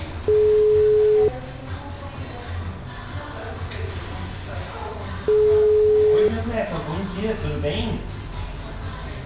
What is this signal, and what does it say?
Ringback tone (Europe, 425 Hz), with speech.
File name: out- -2002- - - .4730